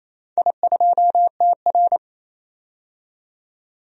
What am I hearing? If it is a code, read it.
Morse code, heard as a keyed tone, about 28 wpm, 700 Hz. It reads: I2TR
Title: DXCC-100 Call Signs - 4 Characters - 28wpm